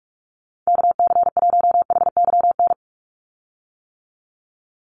Morse code, heard as a keyed tone, about 44 wpm, 700 Hz. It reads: KC1HYN
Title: USA Call Signs 45wpm